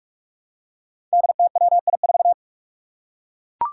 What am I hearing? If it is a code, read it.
Morse code, heard as a keyed tone, about 44 wpm, 700 Hz. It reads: DTWI4
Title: Five Challenging Letter-Number Combinations 45wpm